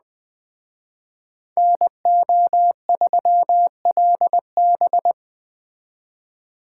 Morse code, heard as a keyed tone, about 20 wpm, 700 Hz. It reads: NO3LB
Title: Speed Racing - USA Call Signs 17wpm